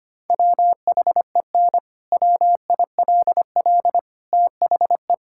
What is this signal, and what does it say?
Morse code, heard as a keyed tone, about 25 wpm, 700 Hz. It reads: WHEN WILL THE
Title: Sentences from Top 1000 Words - Review Part 6 - 25wpm